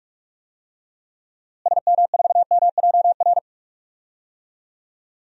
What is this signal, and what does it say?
Morse code, heard as a keyed tone, about 42 wpm, 690 Hz. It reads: SM4MJR
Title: DXCC-100 Call Signs - 6 Characters - Repeated 2x - 45wpm